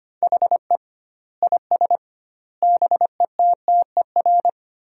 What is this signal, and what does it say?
Morse code, heard as a keyed tone, about 25 wpm, 700 Hz. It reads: HE IS BETTER
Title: 2x Word-Spacing - Sentences from Top 400 Words - 25wpm